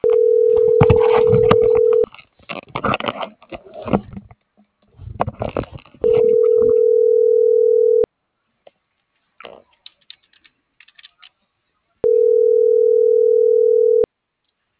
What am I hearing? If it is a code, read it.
Ringback tone (North America), 440 + 480 Hz.